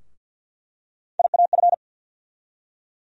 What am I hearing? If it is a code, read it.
Morse code, heard as a keyed tone, about 48 wpm, 700 Hz. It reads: IAF
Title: USA Call Sign Postfixes - 3 Letters 50wpm